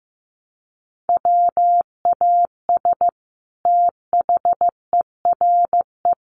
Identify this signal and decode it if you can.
Morse code, heard as a keyed tone, about 15 wpm, 700 Hz. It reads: WAS THERE
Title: Sentences from Top 800 Words 15wpm